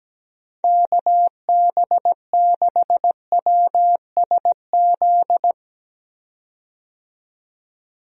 Morse code, heard as a keyed tone, about 17 wpm, 700 Hz. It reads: KB6WSZ
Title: USA Call Signs - 6 Characters - 17wpm